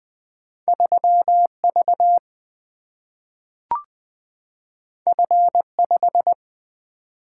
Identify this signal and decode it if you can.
Morse code, heard as a keyed tone, about 20 wpm, 700 Hz. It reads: 3V F5
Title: Single-Letter Single-Number 20wpm